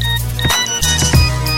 There is no code.